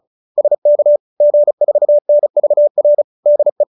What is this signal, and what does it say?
Morse code, heard as a keyed tone, about 34 wpm, 570 Hz. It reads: SK G4NVR DE